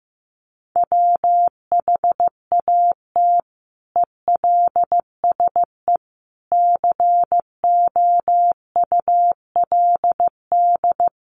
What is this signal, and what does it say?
Morse code, heard as a keyed tone, about 15 wpm, 700 Hz. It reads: WHAT ELSE COULD
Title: Sentences from Top 800 Words 15wpm